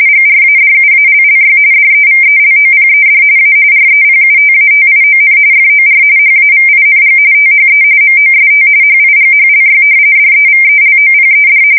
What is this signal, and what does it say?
RTTY, about 45.45 baud. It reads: GRADUALLY DEVELOPED A REPUTATION FOR HAVING THE MOST HIGHLY DEVELOPED
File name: fsk170 / usnavy5